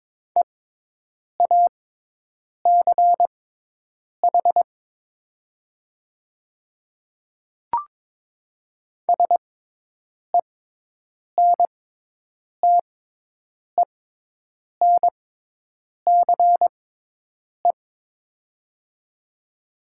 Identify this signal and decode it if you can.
Morse code, heard as a keyed tone, about 22 wpm, spaced as for 8 wpm, 700 Hz. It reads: EACH SENTENCE
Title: 6x Character Spacing - Top 300 Words - 22wpm